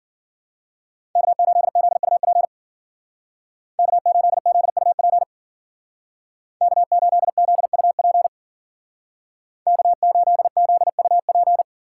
Morse code, heard as a keyed tone, about 52 wpm, 700 Hz. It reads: K8ZUP K8ZUP K8ZUP K8ZUP